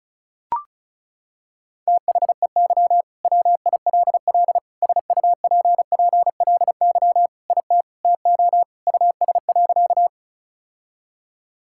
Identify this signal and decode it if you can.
Morse code, heard as a keyed tone, about 34 wpm, 700 Hz. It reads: THEY WILL SUPPLY IT TO US.
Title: Sentences from Top 1000 Words - Review Part 5 - 35wpm